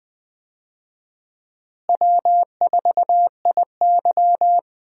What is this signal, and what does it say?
Morse code, heard as a keyed tone, about 21 wpm, 700 Hz. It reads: W4IY
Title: USA Call Signs - 4 Characters - Repeated 3x - 20wpm